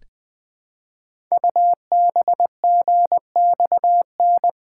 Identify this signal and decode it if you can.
Morse code, heard as a keyed tone, about 20 wpm, 700 Hz. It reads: UBGXN
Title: Code Groups - 5 Letters - 20wpm